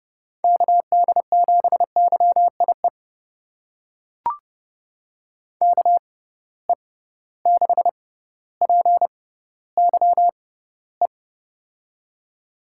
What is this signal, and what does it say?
Morse code, heard as a keyed tone, about 32 wpm, spaced as for 16 wpm, 700 Hz. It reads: KD7YIE KE6PYE
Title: USA Call Signs - 6 Characters - 6x Character Spacing - 30wpm